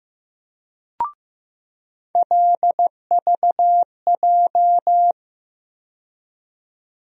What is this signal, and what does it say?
Morse code, heard as a keyed tone, about 15 wpm, 700 Hz. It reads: LVJ
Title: USA Call Sign Postfixes - 3 Letters 15wpm